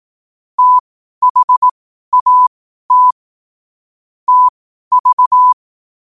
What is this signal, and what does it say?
Morse code, heard as a keyed tone, about 18 wpm, spaced as for 12 wpm, 1000 Hz. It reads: THAT TV